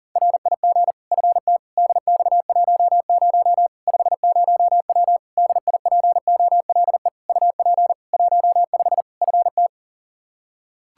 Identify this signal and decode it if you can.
Morse code, heard as a keyed tone, about 40 wpm, 700 Hz. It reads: RIG FT DX10 50W DIPOLE UP 15 FT
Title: K1USN SST QSO Practice 40wpm